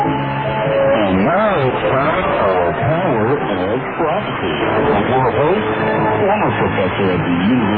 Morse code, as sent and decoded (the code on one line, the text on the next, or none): none